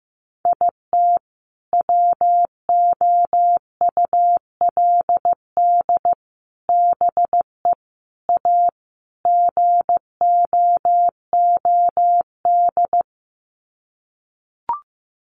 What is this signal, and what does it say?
Morse code, heard as a keyed tone, about 15 wpm, 700 Hz. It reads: IT WOULD BE A GOOD
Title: Sets of 5 Words Encore2 15wpm